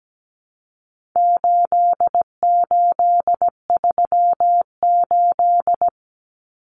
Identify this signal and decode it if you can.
Morse code, heard as a keyed tone, about 17 wpm, 700 Hz. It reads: 8838
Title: Numbers - 4-Digit 17wpm